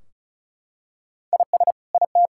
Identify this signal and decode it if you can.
Morse code, heard as a keyed tone, about 36 wpm, 700 Hz. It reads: IS IT
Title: Sentences from Top 1600 Words 35wpm